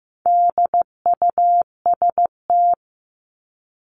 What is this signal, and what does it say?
Morse code, heard as a keyed tone, about 15 wpm, 700 Hz. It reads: DUST
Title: Top 1751-2000 Words 15wpm